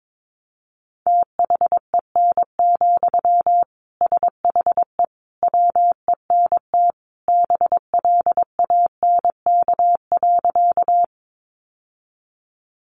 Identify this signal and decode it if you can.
Morse code, heard as a keyed tone, about 22 wpm, 700 Hz. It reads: THEN, SHE WENT BLANK.
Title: Sentences from Top 1500 Words 22wpm